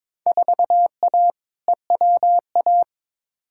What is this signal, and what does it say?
Morse code, heard as a keyed tone, about 22 wpm, 700 Hz. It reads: 4A EWA
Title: Field Day Exchanges - 22wpm